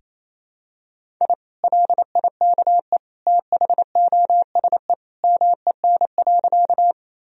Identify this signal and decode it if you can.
Morse code, heard as a keyed tone, about 28 wpm, 700 Hz. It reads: I LIKE THOSE MEN.
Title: Sentences from Top 1000 Words - Review Part 1 - 28wpm